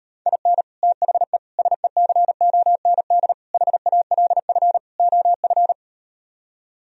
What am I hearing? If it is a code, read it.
Morse code, heard as a keyed tone, about 39 wpm, 700 Hz. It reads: IN THE SECOND HALF OF